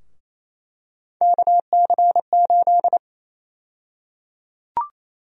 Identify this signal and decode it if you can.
Morse code, heard as a keyed tone, about 27 wpm, 700 Hz. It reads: KC8
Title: USA Call Sign Prefixes - 28wpm